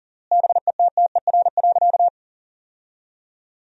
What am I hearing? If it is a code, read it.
Morse code, heard as a keyed tone, about 37 wpm, 700 Hz. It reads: BETTER.
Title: 3x Word-Spacing - Sentences from Top 400 Words - 40wpm